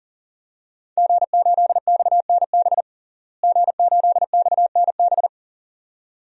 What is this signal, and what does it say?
Morse code, heard as a keyed tone, about 37 wpm, 700 Hz. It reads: G8XNB G8XNB
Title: DXCC-100 Call Signs - 5 Characters - Repeated 2x - 40wpm